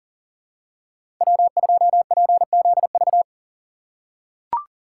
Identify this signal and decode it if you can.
Morse code, heard as a keyed tone, about 42 wpm, 700 Hz. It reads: W2PZV
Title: USA Call Signs - 5 Characters - 6x Character Spacing - 40wpm